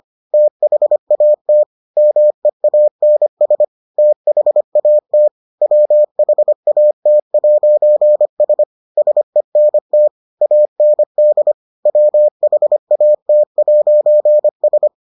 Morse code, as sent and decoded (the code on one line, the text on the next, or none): - .... .- - / -- . .- -. ... / - .... .- - / .-- .... .- - .----. ... / ... -. - / .- -. -.. / .-- .... .- - .----. ...
THAT MEANS THAT WHAT'S SENT AND WHAT'S